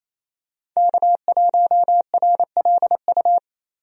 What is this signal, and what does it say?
Morse code, heard as a keyed tone, about 28 wpm, 700 Hz. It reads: K1RLU
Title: USA Call Signs - 5 Characters - Repeated 2x - 28wpm